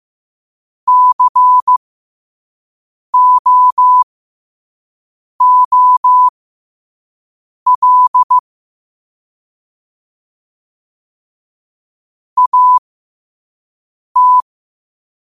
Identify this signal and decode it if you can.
Morse code, heard as a keyed tone, about 15 wpm, spaced as for 5 wpm, 1000 Hz. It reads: COOL AT
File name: qotdfiles / 5wpm